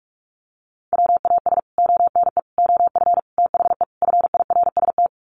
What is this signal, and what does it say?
Morse code, heard as a keyed tone, about 44 wpm, 700 Hz. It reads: WAS ONE OF THE FIRST
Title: Sets of 5 Words 45wpm